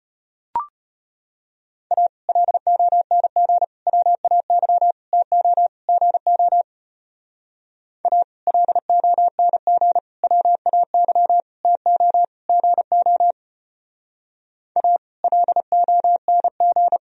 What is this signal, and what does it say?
Morse code, heard as a keyed tone, about 29 wpm, 700 Hz. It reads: A LONG WAY TO GO A LONG WAY TO GO A LONG
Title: Speed-Racing - Sets of 5 Words 25wpm